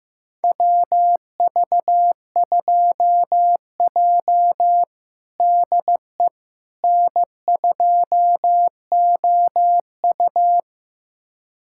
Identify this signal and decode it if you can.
Morse code, heard as a keyed tone, about 15 wpm, 700 Hz. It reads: WV2J DE N2OU
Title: USA Callsign DE Callsign - 4 Characters - 15wpm